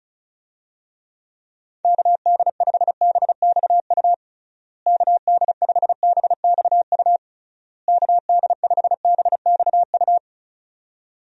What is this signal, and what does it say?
Morse code, heard as a keyed tone, about 34 wpm, 700 Hz. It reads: KD5BXU KD5BXU KD5BXU
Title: USA Call Signs - 6 Characters - Repeated 3x - 35wpm